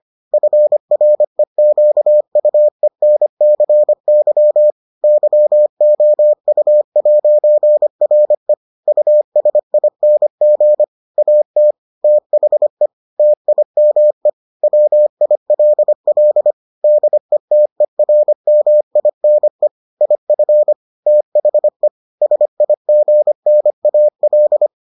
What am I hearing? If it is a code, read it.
Morse code, heard as a keyed tone, about 25 wpm, 600 Hz. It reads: FREQUENCY YOU'RE USING AT THE TIME WILL DETERMINE IF THE SIGNAL